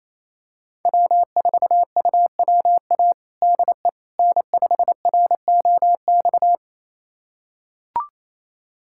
Morse code, heard as a keyed tone, about 28 wpm, 700 Hz. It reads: W4UWA DE N5ROX
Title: USA Callsign DE Callsign - 5 Characters - 28wpm